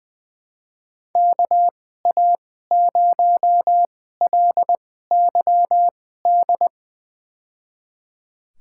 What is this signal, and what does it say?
Morse code, heard as a keyed tone, about 21 wpm, spaced as for 15 wpm, 700 Hz. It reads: KA0LYD
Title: USA Call Signs - 2x Character Spacing - 20wpm